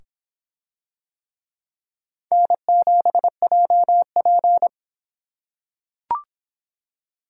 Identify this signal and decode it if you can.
Morse code, heard as a keyed tone, about 25 wpm, 700 Hz. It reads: N7JP